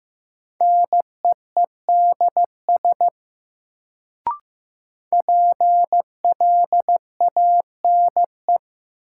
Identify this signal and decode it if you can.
Morse code, heard as a keyed tone, about 16 wpm, 700 Hz. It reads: NEEDS PLANE
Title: Sentences from Top 1000 Words - Review Part 3 - New Word Drill - 15wpm